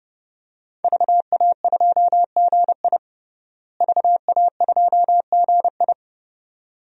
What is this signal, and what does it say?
Morse code, heard as a keyed tone, about 30 wpm, 700 Hz. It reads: VA2GI VA2GI
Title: DXCC-100 Call Signs - 5 Characters - Repeated 2x - 30wpm